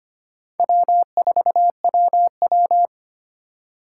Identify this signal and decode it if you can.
Morse code, heard as a keyed tone, about 25 wpm, 700 Hz. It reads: W4WW
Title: USA Call Signs - 4 Characters - 2x Character Spacing - 25wpm